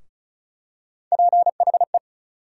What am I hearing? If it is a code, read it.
Morse code, heard as a keyed tone, about 35 wpm, 700 Hz. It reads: PHE
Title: USA Call Sign Postfixes - 3 Letters 35wpm